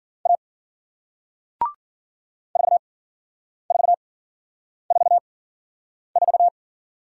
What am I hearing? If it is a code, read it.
Morse code, heard as a keyed tone, about 53 wpm, spaced as for 9 wpm, 700 Hz. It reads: A 4444